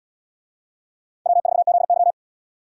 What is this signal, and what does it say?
Morse code, heard as a keyed tone, about 71 wpm, 700 Hz. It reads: W4ZY